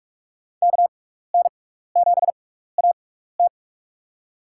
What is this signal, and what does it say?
Morse code, heard as a keyed tone, about 42 wpm, spaced as for 15 wpm, 700 Hz. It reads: KN7AT
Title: USA Call Signs - 5 Characters - 6x Character Spacing - 45wpm